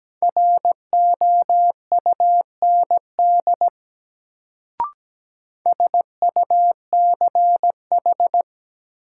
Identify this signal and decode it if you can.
Morse code, heard as a keyed tone, about 17 wpm, 700 Hz. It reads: ROUND SUCH
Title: Sentences from Top 200 Words - New Word Drill - 17wpm